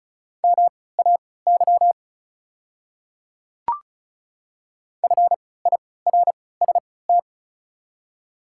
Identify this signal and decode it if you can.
Morse code, heard as a keyed tone, about 36 wpm, spaced as for 20 wpm, 690 Hz. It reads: MAY FIRST
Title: 3x Character Spacing - Top 100 Words - 35wpm